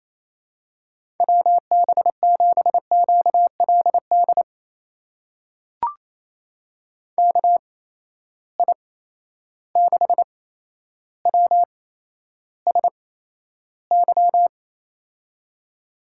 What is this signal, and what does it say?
Morse code, heard as a keyed tone, about 28 wpm, spaced as for 12 wpm, 700 Hz. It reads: WB7QLD KI6WSY
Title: USA Call Signs - 6 Characters - 8x Character Spacing - 28wpm